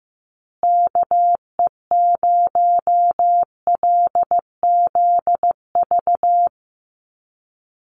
Morse code, heard as a keyed tone, about 15 wpm, 700 Hz. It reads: KE0LZV
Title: USA Call Signs Repeated 2x 15wpm